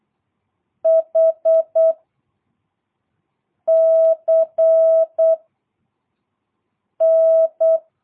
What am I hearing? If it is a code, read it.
Morse code, heard as a keyed tone, about 8 wpm, 660 Hz. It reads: H C N
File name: jh8cbh / 2022cw